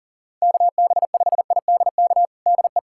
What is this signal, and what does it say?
Morse code, heard as a keyed tone, about 37 wpm, 700 Hz. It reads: KB5IDK DE
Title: USA Callsign DE Callsign - 6 Characters - 40wpm